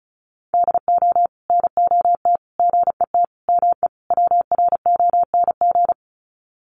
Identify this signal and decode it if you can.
Morse code, heard as a keyed tone, about 34 wpm, 700 Hz. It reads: DO NOT GET ME WRONG
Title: Sets of 5 Words Encore 35wpm